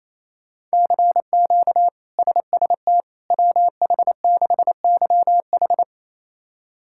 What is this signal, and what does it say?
Morse code, heard as a keyed tone, about 28 wpm, 700 Hz. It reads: CQ SST WH6YH
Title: K1USN SST QSO Practice - 28wpm